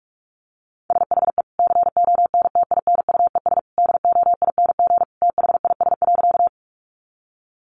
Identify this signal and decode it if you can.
Morse code, heard as a keyed tone, about 44 wpm, 700 Hz. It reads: SHE CONTINUES DOING THIS.
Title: Easy Phrases 45wpm